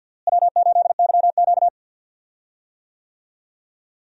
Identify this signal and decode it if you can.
Morse code, heard as a keyed tone, about 51 wpm, 700 Hz. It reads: W8YQ
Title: USA Call Signs - 4 Characters - Repeated 3x - 50wpm